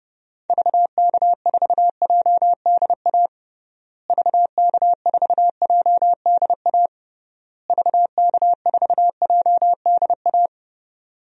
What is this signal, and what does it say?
Morse code, heard as a keyed tone, about 32 wpm, 700 Hz. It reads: VK4JDA VK4JDA VK4JDA
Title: DXCC-100 Call Signs - 6 Characters - Repeated 3x - 30wpm